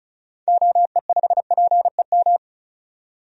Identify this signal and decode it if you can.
Morse code, heard as a keyed tone, about 34 wpm, 700 Hz. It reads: OE5PEM
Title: DXCC-100 Call Signs - 6 Characters - 35wpm